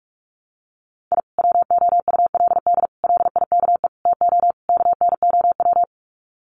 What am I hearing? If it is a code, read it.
Morse code, heard as a keyed tone, about 44 wpm, 700 Hz. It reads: I WOULD LIKE TO KNOW
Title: Sets of 5 Words Encore 45wpm